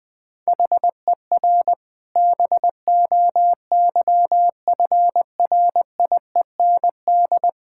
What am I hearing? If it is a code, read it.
Morse code, heard as a keyed tone, about 21 wpm, 700 Hz. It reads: HER BOYFRIEND
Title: Sentences from Top 1000 Words - Review Part 5 - 20wpm